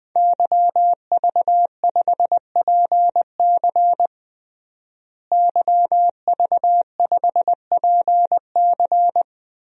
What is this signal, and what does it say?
Morse code, heard as a keyed tone, about 21 wpm, 700 Hz. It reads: YV5PC YV5PC
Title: DXCC-100 Call Signs - 5 Characters - Repeated 3x - 20wpm